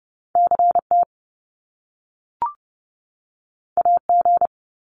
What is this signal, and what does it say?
Morse code, heard as a keyed tone, about 30 wpm, 700 Hz. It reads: CT AG
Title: Top Two Letter Combinations in Words 30wpm